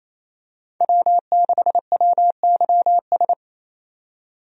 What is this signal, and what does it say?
Morse code, heard as a keyed tone, about 28 wpm, 700 Hz. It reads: W6WYS
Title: USA Call Signs - 5 Characters - 28wpm